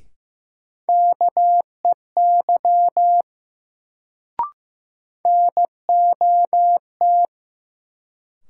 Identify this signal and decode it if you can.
Morse code, heard as a keyed tone, about 15 wpm, 700 Hz. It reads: KEY NOT
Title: Sentences from Top 600 Words - New Word Drill - 15wpm